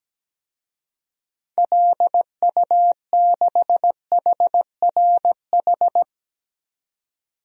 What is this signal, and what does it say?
Morse code, heard as a keyed tone, about 17 wpm, 700 Hz. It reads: LU6HRH
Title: DXCC-100 Call Signs - 6 Characters - 17wpm